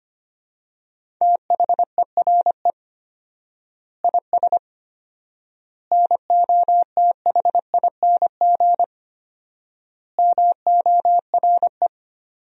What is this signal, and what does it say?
Morse code, heard as a keyed tone, about 25 wpm, 700 Hz. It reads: THERE IS NOTHING MORE
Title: 4x Word-Spacing - Sentences from Top 500 Words - 25wpm